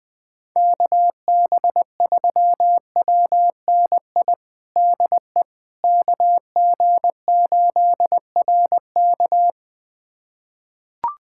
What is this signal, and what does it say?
Morse code, heard as a keyed tone, about 20 wpm, 700 Hz. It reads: KB3WNI DE KG8RK
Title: USA Callsign DE Callsign 20wpm